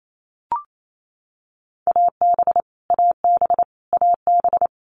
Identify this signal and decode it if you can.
Morse code, heard as a keyed tone, about 28 wpm, 700 Hz. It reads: AB AB AB
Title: Canadian Provinces and Territories - Repeated 3x - 28wpm